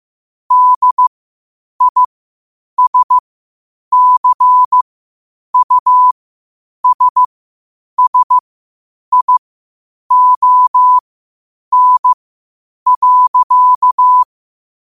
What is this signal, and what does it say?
Morse code, heard as a keyed tone, about 15 wpm, spaced as for 8 wpm, 1000 Hz. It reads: DISCUSSION.
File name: qotdfiles / 8wpm